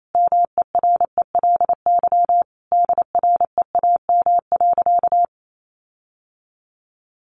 Morse code, heard as a keyed tone, about 28 wpm, 700 Hz. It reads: MERELY DREAM.
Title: Sentences from Top 1500 Words 28wpm